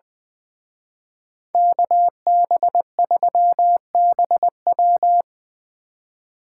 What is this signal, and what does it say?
Morse code, heard as a keyed tone, about 25 wpm, 700 Hz. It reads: KB3BW
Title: Speed Racing - USA Call Signs 20wpm